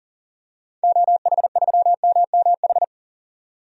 Morse code, heard as a keyed tone, about 40 wpm, 700 Hz. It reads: OH3MMH
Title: DXCC-100 Call Signs - 6 Characters - 40wpm